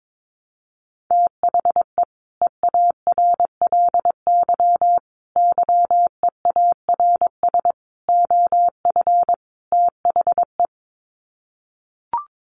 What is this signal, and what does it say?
Morse code, heard as a keyed tone, about 22 wpm, 700 Hz. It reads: THE EARLY YEARS OF THE
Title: Sets of 5 Words Encore 22wpm